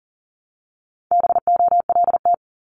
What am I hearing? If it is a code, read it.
Morse code, heard as a keyed tone, about 39 wpm, 700 Hz. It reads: BOLT
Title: Multi-letter Phonemes 40wpm